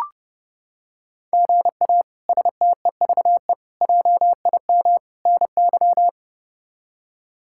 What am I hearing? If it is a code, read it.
Morse code, heard as a keyed tone, about 30 wpm, 700 Hz. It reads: GA STEVE JIM NY
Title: K1USN SST QSO Practice - 30wpm